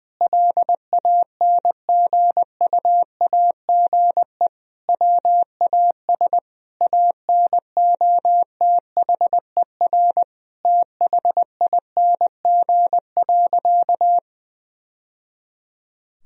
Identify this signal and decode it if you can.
Morse code, heard as a keyed tone, about 20 wpm, 700 Hz. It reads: LANGUAGE WAS ANOTHER THING.
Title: Sentences from Top 1000 Words - Review Part 3 - 20wpm